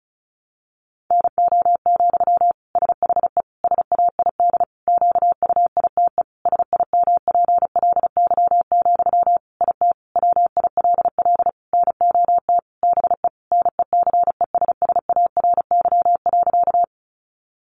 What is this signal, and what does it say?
Morse code, heard as a keyed tone, about 34 wpm, 700 Hz. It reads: NO, SHE SAID QUITE SIMPLY, IT WILL NOT BE NECESSARY.
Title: Sentences from Top 800 Words 35wpm